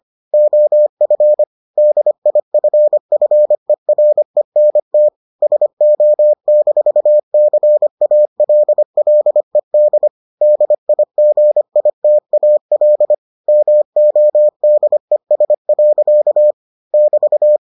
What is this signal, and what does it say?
Morse code, heard as a keyed tone, about 25 wpm, 600 Hz. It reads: OF DIFFERENT SO-CALLED DIGITAL MODES. <BT>